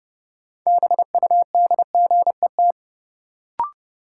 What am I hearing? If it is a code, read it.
Morse code, heard as a keyed tone, about 30 wpm, 700 Hz. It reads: BUDGET
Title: Sentences from Top 1200 Words - New Word Drill Alt - 30wpm